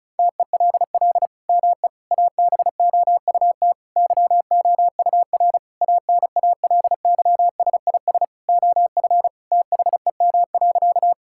Morse code, heard as a keyed tone, about 34 wpm, 700 Hz. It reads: TELL ME ABOUT YOUR ANALYSIS OF THEM.
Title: Sentences from Top 1000 Words - Review Part 5 - 35wpm